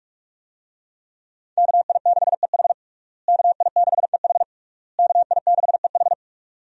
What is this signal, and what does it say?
Morse code, heard as a keyed tone, about 46 wpm, 700 Hz. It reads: KI6EH KI6EH KI6EH
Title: USA Call Signs - 5 Characters - Repeated 3x - 45wpm